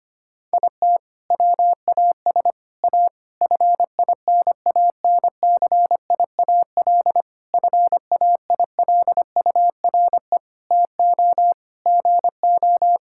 Morse code, heard as a keyed tone, about 25 wpm, 700 Hz. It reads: IT WAS A FINANCIAL FAILURE TO GO
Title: Sentences from Top 1000 Words - Review Part 6 - 25wpm